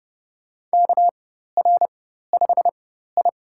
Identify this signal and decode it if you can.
Morse code, heard as a keyed tone, about 30 wpm, spaced as for 14 wpm, 700 Hz. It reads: KR5I
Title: USA Call Signs - 4x Character Spacing - 30wpm